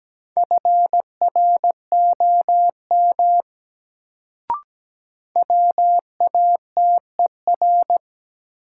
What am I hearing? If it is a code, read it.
Morse code, heard as a keyed tone, about 17 wpm, 700 Hz. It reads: FROM WATER
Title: Sentences from Top 100 Words - New Word Drill - 17wpm